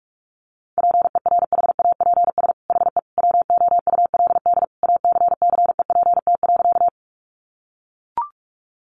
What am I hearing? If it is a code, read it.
Morse code, heard as a keyed tone, about 44 wpm, 700 Hz. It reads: PERHAPS HE WOULD ACCEPT.
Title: Sentences from Top 1300 Words 45wpm